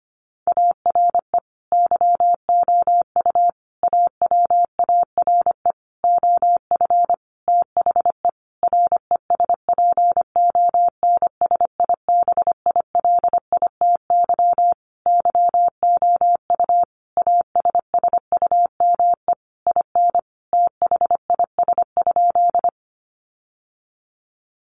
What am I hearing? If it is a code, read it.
Morse code, heard as a keyed tone, about 25 wpm, 700 Hz. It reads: ARE YOU AWARE OF THE RESPONSIBILITY YOU ASSUME IN THIS?
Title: Sentences from Top 1500 Words 25wpm